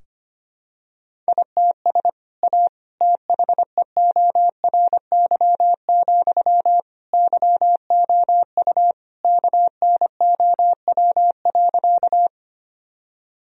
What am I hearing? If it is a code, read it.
Morse code, heard as a keyed tone, about 25 wpm, 700 Hz. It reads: ITS A THEORY, YOU KNOW.
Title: Sentences from Top 1000 Words - Review Part 3 - 25wpm